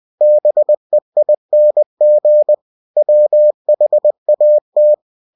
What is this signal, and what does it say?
Morse code, heard as a keyed tone, about 20 wpm, 600 Hz. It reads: BEING WHAT
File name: cw-books / winnie-the-pooh / winnie-20wpm-page-0101